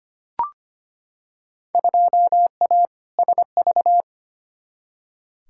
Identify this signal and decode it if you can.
Morse code, heard as a keyed tone, about 25 wpm, 700 Hz. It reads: 2A SV
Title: Field Day Exchanges - 25wpm